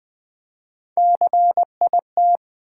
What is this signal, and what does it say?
Morse code, heard as a keyed tone, about 20 wpm, 700 Hz. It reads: CIT
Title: Build A Word - 1 Letter At A Time - 3 to 5-Letter Words - 20wpm